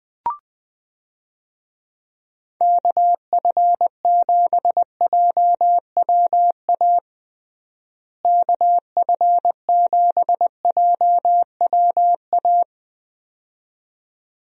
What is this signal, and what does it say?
Morse code, heard as a keyed tone, about 20 wpm, 700 Hz. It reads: KF7JWA KF7JWA
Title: USA Call Signs - 6 Characters - Repeated 2x - 20wpm